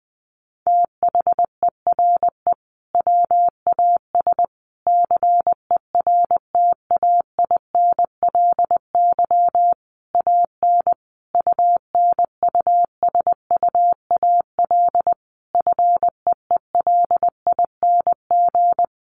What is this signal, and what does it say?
Morse code, heard as a keyed tone, about 20 wpm, 700 Hz. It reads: THERE WAS CERTAINLY AN UNUSUAL FEELING
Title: Sentences from Top 1400 Words 20wpm